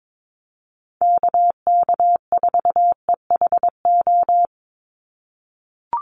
Morse code, heard as a keyed tone, about 22 wpm, 700 Hz. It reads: KK4EHO
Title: USA Call Signs 22wpm